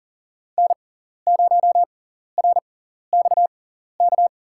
Morse code, heard as a keyed tone, about 40 wpm, spaced as for 14 wpm, 700 Hz. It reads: N0RXK
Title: USA Call Signs - 6x Character Spacing - 40wpm